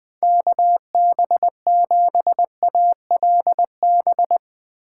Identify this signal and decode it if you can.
Morse code, heard as a keyed tone, about 20 wpm, 700 Hz. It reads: KB7ALB
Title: USA Call Signs - 6 Characters - 4x Character Spacing - 20wpm